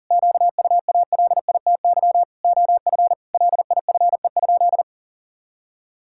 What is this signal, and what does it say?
Morse code, heard as a keyed tone, about 40 wpm, 700 Hz. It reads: QUALITY OF LIFE?
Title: Sentences from Top 1000 Words - Review Part 5 - 40wpm